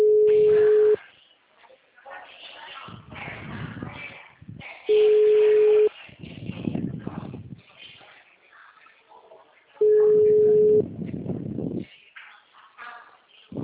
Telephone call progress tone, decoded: ringback tone

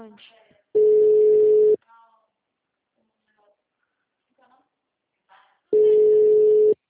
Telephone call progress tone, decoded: ringback tone

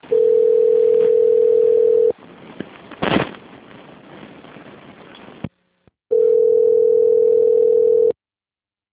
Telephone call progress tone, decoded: ringback tone

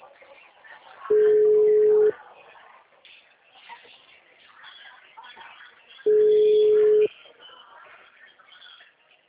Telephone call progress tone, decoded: ringback tone

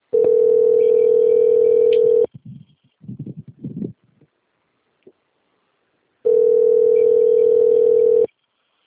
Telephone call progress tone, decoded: ringback tone

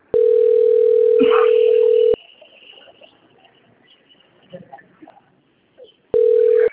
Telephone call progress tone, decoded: ringback tone

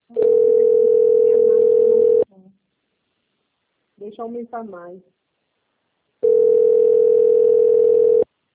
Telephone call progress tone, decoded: ringback tone

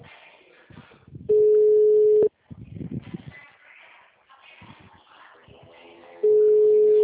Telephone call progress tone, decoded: ringback tone

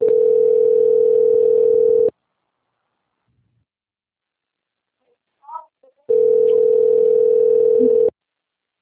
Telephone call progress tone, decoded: ringback tone

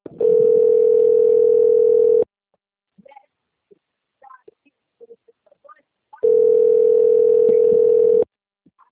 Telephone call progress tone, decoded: ringback tone

